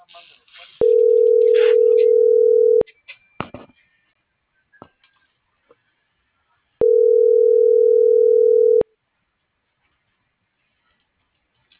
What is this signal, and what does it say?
Ringback tone (North America), 440 + 480 Hz.